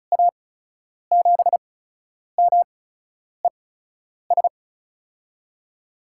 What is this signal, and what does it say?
Morse code, heard as a keyed tone, about 36 wpm, spaced as for 10 wpm, 700 Hz. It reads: A7MES
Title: USA Call Signs - 6 Characters - 8x Character Spacing - 35wpm